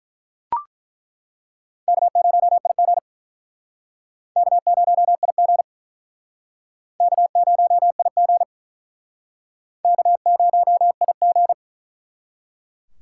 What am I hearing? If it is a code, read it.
Morse code, heard as a keyed tone, about 48 wpm, 700 Hz. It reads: K0IG K0IG K0IG K0IG